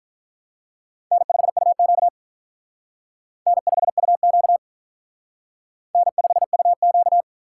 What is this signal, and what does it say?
Morse code, heard as a keyed tone, about 45 wpm, 700 Hz. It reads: N5UQ N5UQ N5UQ